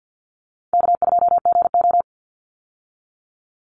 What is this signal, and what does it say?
Morse code, heard as a keyed tone, about 50 wpm, 700 Hz. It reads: K2GO
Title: Call Signs Repeated 3x - 50wpm